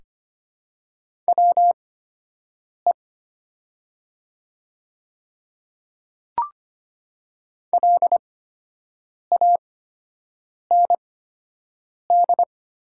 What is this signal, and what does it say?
Morse code, heard as a keyed tone, about 25 wpm, spaced as for 7 wpm, 700 Hz. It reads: WE LAND
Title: 8x Character Spacing - Top 200 Words - 25wpm